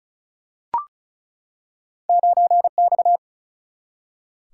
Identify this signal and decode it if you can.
Morse code, heard as a keyed tone, about 36 wpm, 700 Hz. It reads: 9X
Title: Single-Letter Single-Number 35wpm